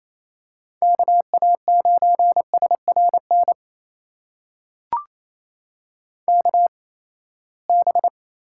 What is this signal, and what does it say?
Morse code, heard as a keyed tone, about 27 wpm, spaced as for 19 wpm, 700 Hz. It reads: KA9SRN KB
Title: USA Call Signs - 6 Characters - 8x Character Spacing - 28wpm